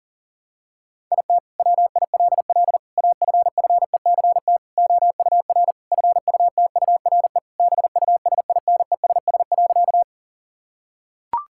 Morse code, heard as a keyed tone, about 40 wpm, 700 Hz. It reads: IT WILL AFFECT OUR FUTURE BUSINESS.
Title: Sentences from Top 1000 Words - Review Part 6 - 40wpm